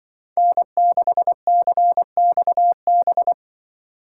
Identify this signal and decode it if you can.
Morse code, heard as a keyed tone, about 25 wpm, 700 Hz. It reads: N6CXB
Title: Speed Racing - USA 1x3 Call Signs 20wpm